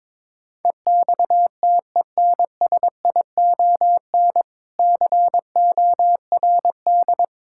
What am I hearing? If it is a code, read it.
Morse code, heard as a keyed tone, about 22 wpm, 690 Hz. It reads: EXTENSION CORD
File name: Difficult Words with Hints - 22wpm